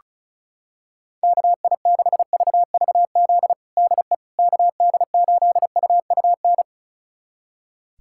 Morse code, heard as a keyed tone, about 34 wpm, 700 Hz. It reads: KI6VVZ DE KD8UUN
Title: USA Callsign DE Callsign - 6 Characters - 35wpm